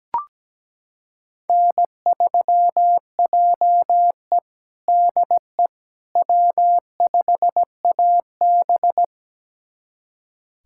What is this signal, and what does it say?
Morse code, heard as a keyed tone, about 17 wpm, 700 Hz. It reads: N3JE DE W5AB
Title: USA Callsign DE Callsign - 4 Characters - 17wpm